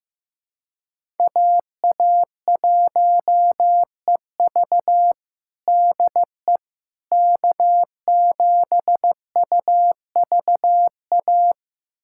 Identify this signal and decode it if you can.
Morse code, heard as a keyed tone, about 16 wpm, 700 Hz. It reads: AA1EV DE K7UVA
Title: USA Callsign DE Callsign - 5 Characters - 15wpm